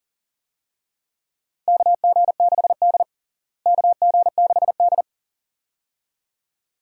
Morse code, heard as a keyed tone, about 40 wpm, 700 Hz. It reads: KG6D KG6D
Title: USA Call Signs - 4 Characters - Repeated 2x - 40wpm